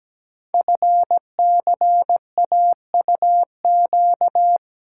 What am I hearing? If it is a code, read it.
Morse code, heard as a keyed tone, about 17 wpm, 700 Hz. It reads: FCAUQ
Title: Code Groups - 5 Letters - 17wpm